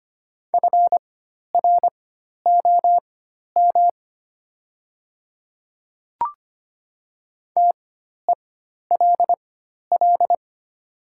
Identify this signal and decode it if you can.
Morse code, heard as a keyed tone, about 25 wpm, spaced as for 12 wpm, 700 Hz. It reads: FROM TELL
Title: 4x Character Spacing - Top 200 Words - 25wpm